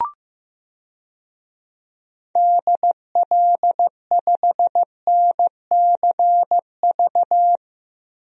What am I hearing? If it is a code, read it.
Morse code, heard as a keyed tone, about 16 wpm, 700 Hz. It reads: DL5NCV
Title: DXCC-100 Call Signs - 6 Characters - 15wpm